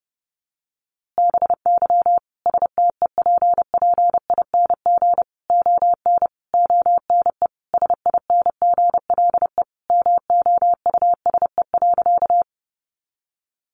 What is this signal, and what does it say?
Morse code, heard as a keyed tone, about 30 wpm, 700 Hz. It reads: BY STEPPING ON ONE SINGLE MOUSE.
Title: A Sound Of Thunder 30wpm